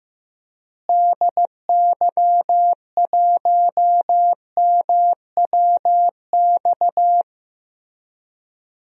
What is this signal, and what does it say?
Morse code, heard as a keyed tone, about 15 wpm, 700 Hz. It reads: DY1MWX
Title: DXCC-100 Call Signs - 6 Characters - Repeated 3x - 15wpm